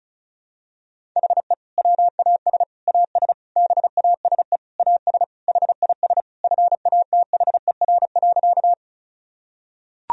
Morse code, heard as a keyed tone, about 34 wpm, 700 Hz. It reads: HE WAS AS BASE AS HIS FATHER.
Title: Sentences from Top 1000 Words - Review Part 1 - 35wpm